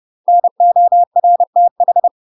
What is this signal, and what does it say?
Morse code, heard as a keyed tone, about 30 wpm, 700 Hz. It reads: NORTH
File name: UK MASTER QSOs TRAINING / 30WPM / UK MASTER QSOs 0033